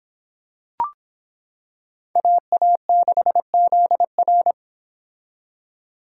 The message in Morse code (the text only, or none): AA6ZR